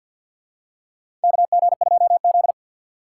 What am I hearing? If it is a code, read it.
Morse code, heard as a keyed tone, about 53 wpm, 700 Hz. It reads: KG2Z
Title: USA Call Signs - 4 Characters - 6x Character Spacing - 50wpm